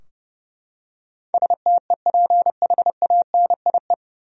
Morse code, heard as a keyed tone, about 30 wpm, 700 Hz. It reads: STEPHANIE
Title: USA Common Names - 30wpm